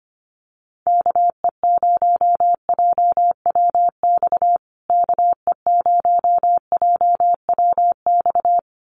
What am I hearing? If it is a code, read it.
Morse code, heard as a keyed tone, about 25 wpm, 700 Hz. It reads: KE0JWX KE0JWX
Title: USA Call Signs Repeated 2x 25wpm